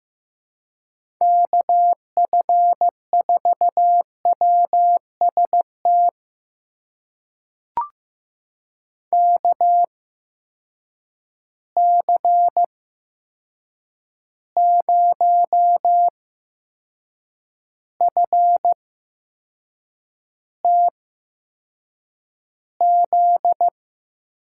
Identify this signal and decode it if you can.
Morse code, heard as a keyed tone, about 15 wpm, spaced as for 6 wpm, 700 Hz. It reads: KF4WST KC0FTZ
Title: USA Call Signs - 6 Characters - 8x Character Spacing - 15wpm